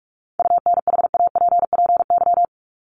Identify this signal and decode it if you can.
Morse code, heard as a keyed tone, about 44 wpm, 700 Hz. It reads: UNHAPPY
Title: Top 1501-1750 Words 45wpm